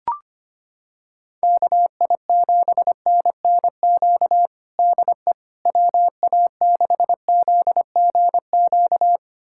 Morse code, heard as a keyed tone, about 25 wpm, 700 Hz. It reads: KI7NNQ DE WA6ZGQ
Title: USA Callsign DE Callsign - 6 Characters - 25wpm